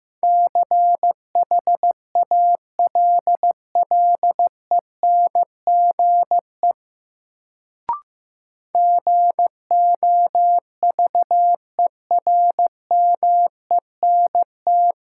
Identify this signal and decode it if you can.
Morse code, heard as a keyed tone, about 15 wpm, 700 Hz. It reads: CHALLENGE GOVERMENT
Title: Sentences from Top 1300 Words - New Word Drill Alt - 15wpm